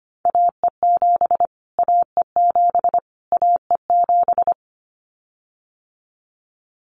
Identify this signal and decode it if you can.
Morse code, heard as a keyed tone, about 25 wpm, 700 Hz. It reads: AE7 AE7 AE7
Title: Callsign Prefixes for Top 25 DXCC - Repeated 3x - 25wpm